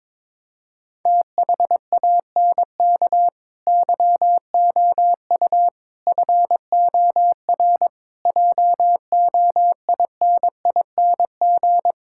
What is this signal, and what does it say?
Morse code, heard as a keyed tone, about 22 wpm, 700 Hz. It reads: THANK YOU FOR JOINING
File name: Build A Phrase - 5-Word Phrases - 22wpm